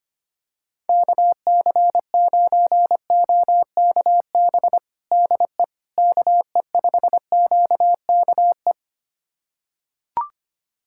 Morse code, heard as a keyed tone, about 25 wpm, 700 Hz. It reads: KC9OKB DE KE5QKE
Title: USA Callsign DE Callsign - 6 Characters - 25wpm